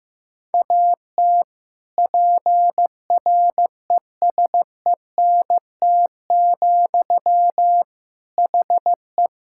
Morse code, heard as a keyed tone, about 15 wpm, 700 Hz. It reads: AT PRESENT, HE
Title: Sentences from Top 1000 Words - Review Part 5 - 15wpm